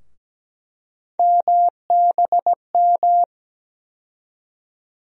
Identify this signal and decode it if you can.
Morse code, heard as a keyed tone, about 17 wpm, 700 Hz. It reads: MBM
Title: USA Call Sign Postfixes - 3 Letters 17wpm